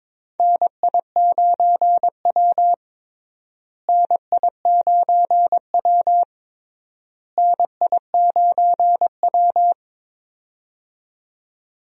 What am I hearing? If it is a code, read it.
Morse code, heard as a keyed tone, about 21 wpm, 700 Hz. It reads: NI9W NI9W NI9W
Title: USA Call Signs - 4 Characters - Repeated 3x - 22wpm